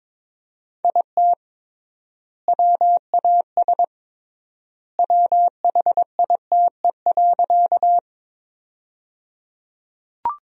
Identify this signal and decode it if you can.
Morse code, heard as a keyed tone, about 22 wpm, 700 Hz. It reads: IT WAS WHITE.
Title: 3x Word-Spacing - Sentences from Top 300 Words - 22wpm